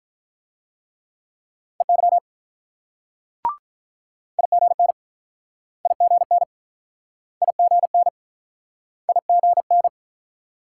Morse code, heard as a keyed tone, about 47 wpm, 700 Hz. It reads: EY IGN IGN IGN IGN